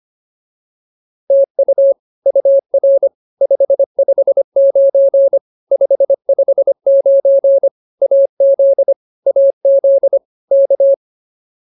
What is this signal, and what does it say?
Morse code, heard as a keyed tone, about 24 wpm, 550 Hz. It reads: TU UR 559 559 AZ AZ K
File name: POTA310a 25